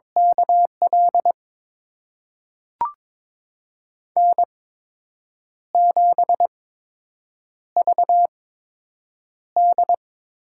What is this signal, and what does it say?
Morse code, heard as a keyed tone, about 22 wpm, spaced as for 7 wpm, 700 Hz. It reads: KL N7VD
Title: USA Call Signs - 8x Character Spacing - 22wpm